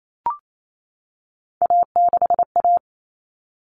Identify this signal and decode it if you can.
Morse code, heard as a keyed tone, about 28 wpm, 700 Hz. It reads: A6A
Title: Common Cut Numbers - 3-Digit 28wpm